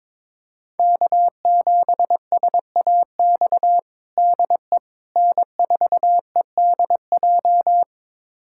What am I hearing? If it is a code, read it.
Morse code, heard as a keyed tone, about 22 wpm, 700 Hz. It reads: K7SAX DE N4EDJ
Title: USA Callsign DE Callsign - 5 Characters - 22wpm